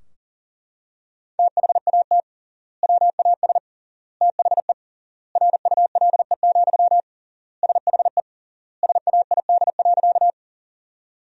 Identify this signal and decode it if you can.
Morse code, heard as a keyed tone, about 40 wpm, 700 Hz. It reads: THAT WAS THE RULE, SHE SAID.
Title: 3x Word-Spacing - Sentences from Top 400 Words - 40wpm